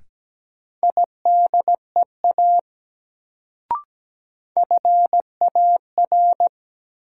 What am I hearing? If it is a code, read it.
Morse code, heard as a keyed tone, about 17 wpm, 700 Hz. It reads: IDEA FAR
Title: Sets of 3 Words - New Word Drill - 17wpm